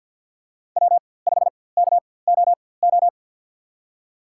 Morse code, heard as a keyed tone, about 48 wpm, spaced as for 23 wpm, 700 Hz. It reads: W5KOO
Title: USA Call Signs - 5 Characters - 4x Character Spacing - 50wpm